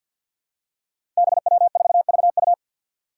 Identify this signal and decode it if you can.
Morse code, heard as a keyed tone, about 46 wpm, 700 Hz. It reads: BW4VU
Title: Five Challenging Letter-Number Combinations 50wpm